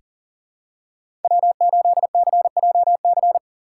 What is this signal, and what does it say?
Morse code, heard as a keyed tone, about 40 wpm, 700 Hz. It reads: W8CJC
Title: USA Call Signs - 5 Characters - 4x Character Spacing - 40wpm